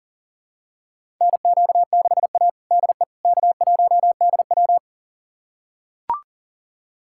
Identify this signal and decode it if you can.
Morse code, heard as a keyed tone, about 40 wpm, 700 Hz. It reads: NQ6A DE K1DW
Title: USA Callsign DE Callsign - 4 Characters - 40wpm